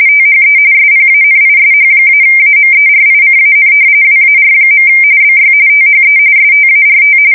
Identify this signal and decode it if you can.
RTTY, about 45.45 baud. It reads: THE SPANISH-AMERICAN WAR. BY THE START
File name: fsk170 / battleships